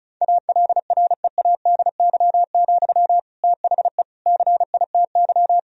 Morse code, heard as a keyed tone, about 34 wpm, 700 Hz. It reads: ALREADY, THE CITY
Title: Sentences from Top 1000 Words - Review Part 4 - 35wpm